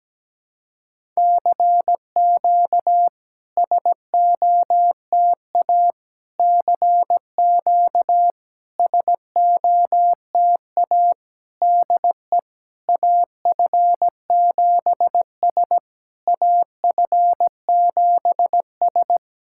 Morse code, heard as a keyed tone, about 17 wpm, 700 Hz. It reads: CQ SOTA CQ SOTA DE AF7S AF7S
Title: SOTA QSO Practice 17wpm